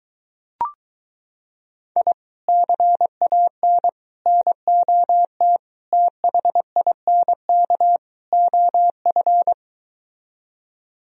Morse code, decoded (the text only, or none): I CAN NOT THINK OF